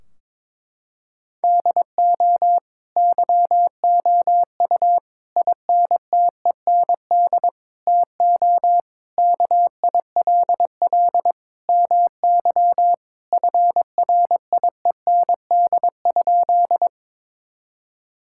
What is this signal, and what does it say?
Morse code, heard as a keyed tone, about 23 wpm, 700 Hz. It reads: DO YOU INTEND TO KILL MY FRIEND?
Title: Sentences from Top 1600 Words 22wpm